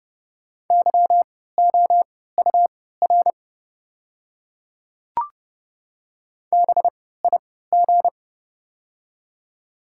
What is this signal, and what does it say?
Morse code, heard as a keyed tone, about 29 wpm, spaced as for 17 wpm, 700 Hz. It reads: YOUR BIG
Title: 3x Character Spacing - Top 200 Words - 30wpm